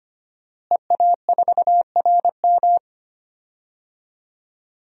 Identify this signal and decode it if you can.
Morse code, heard as a keyed tone, about 25 wpm, 700 Hz. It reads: EA4RM
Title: DXCC-100 Call Signs - 5 Characters - Repeated 3x - 25wpm